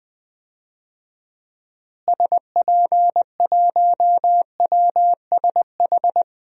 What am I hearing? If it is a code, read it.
Morse code, heard as a keyed tone, about 20 wpm, 700 Hz. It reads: SP1WSH
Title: DXCC-100 Call Signs - 6 Characters - Repeated 3x - 20wpm